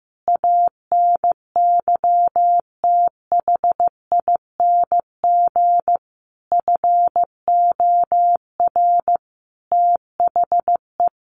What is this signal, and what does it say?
Morse code, heard as a keyed tone, about 15 wpm, 700 Hz. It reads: ANYTHING FOR THE
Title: Sentences from Top 1500 Words 15wpm